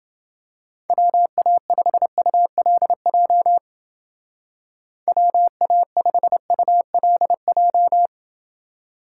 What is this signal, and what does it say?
Morse code, heard as a keyed tone, about 28 wpm, 700 Hz. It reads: WA5ULJ WA5ULJ